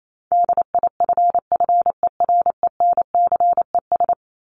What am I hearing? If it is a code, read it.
Morse code, heard as a keyed tone, about 28 wpm, 700 Hz. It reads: DIFFERENCES
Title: ES:S Suffix - Top 100 Words with Base Words 28wpm